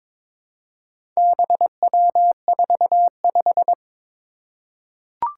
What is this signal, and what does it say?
Morse code, heard as a keyed tone, about 22 wpm, 700 Hz. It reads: BW45
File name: practice / 22wpm / Four Letter-Number Combinations - 22wpm